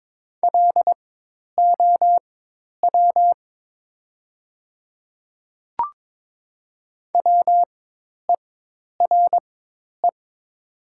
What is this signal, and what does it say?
Morse code, heard as a keyed tone, about 22 wpm, spaced as for 10 wpm, 700 Hz. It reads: LOW WERE
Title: 4x Character Spacing - Top 200 Words - 22wpm